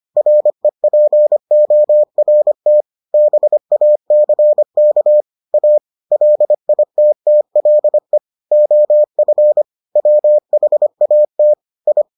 Morse code, heard as a keyed tone, about 25 wpm, 600 Hz. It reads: REPORT BACK A LITTLE OF WHAT I